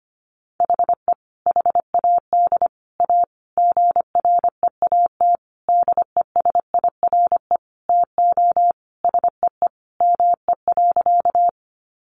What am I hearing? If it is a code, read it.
Morse code, heard as a keyed tone, about 25 wpm, 700 Hz. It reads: HE HAD A GREAT DESIRE TO SEE ME.
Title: Sentences from Top 1100 Words 25wpm